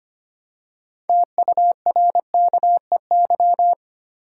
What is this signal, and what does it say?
Morse code, heard as a keyed tone, about 25 wpm, 700 Hz. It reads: TURKEY
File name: practice / 25wpm / Easy Words with Hints - 25wpm